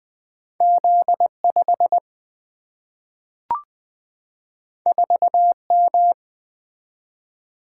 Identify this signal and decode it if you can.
Morse code, heard as a keyed tone, about 20 wpm, 700 Hz. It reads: Z 5 4 M
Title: Single-Letter Single-Number 20wpm